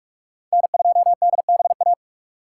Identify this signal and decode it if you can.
Morse code, heard as a keyed tone, about 59 wpm, 700 Hz. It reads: N2DBA